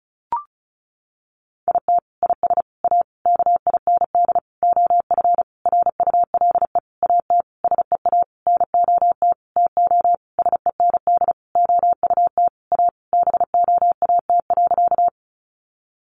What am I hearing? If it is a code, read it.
Morse code, heard as a keyed tone, about 34 wpm, 700 Hz. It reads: IT IS A KIND OF RULE AT SEA NOT TO SEND OUT A BOAT.
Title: Sentences from Top 1500 Words 35wpm